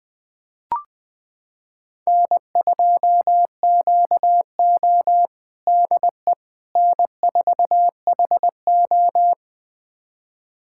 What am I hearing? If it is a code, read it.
Morse code, heard as a keyed tone, about 20 wpm, 700 Hz. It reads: N2QO DE N4HO
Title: USA Callsign DE Callsign - 4 Characters - 20wpm